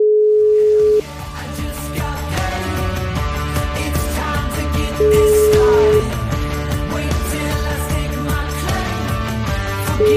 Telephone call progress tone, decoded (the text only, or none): ringback tone